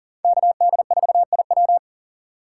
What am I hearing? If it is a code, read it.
Morse code, heard as a keyed tone, about 40 wpm, 700 Hz. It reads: KD4IW
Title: USA Call Signs - 5 Characters - 6x Character Spacing - 40wpm